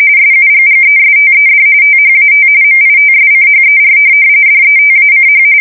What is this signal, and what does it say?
RTTY, about 45.45 baud. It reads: SUGGESTED THAT MITSCHER DETACH
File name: fsk170 / USS-Massachusetts